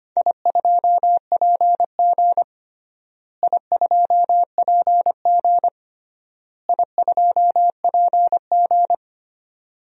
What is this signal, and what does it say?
Morse code, heard as a keyed tone, about 25 wpm, 700 Hz. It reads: I2PG I2PG I2PG
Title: DXCC-100 Call Signs - 4 Characters - Repeated 3x - 25wpm